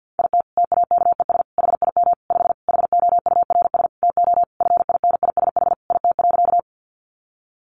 Morse code, heard as a keyed tone, about 50 wpm, 700 Hz. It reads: IT TAKES HIM 5 HOURS TO FINISH IT.
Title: Additional Easy Phrases 50wpm